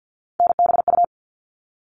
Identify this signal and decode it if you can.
Morse code, heard as a keyed tone, about 48 wpm, 700 Hz. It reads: NBU
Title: All Cut Numbers - 3-Digit 50wpm